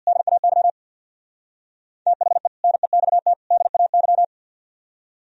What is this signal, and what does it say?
Morse code, heard as a keyed tone, about 60 wpm, 700 Hz. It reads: DAY THE NEXT DAY
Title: Speed-Racing - Sets of 3 Words 50wpm